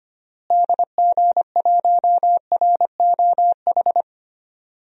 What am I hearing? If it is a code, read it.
Morse code, heard as a keyed tone, about 25 wpm, 700 Hz. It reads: DG1ROH
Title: DXCC-100 Call Signs - 6 Characters - Repeated 2x - 25wpm